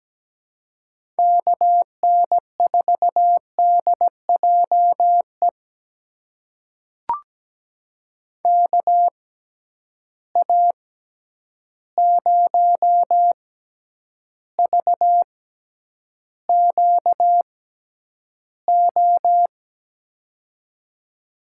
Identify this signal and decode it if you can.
Morse code, heard as a keyed tone, about 17 wpm, spaced as for 9 wpm, 700 Hz. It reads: KN4DJE KA0VQO
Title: USA Call Signs - 6 Characters - 6x Character Spacing - 17wpm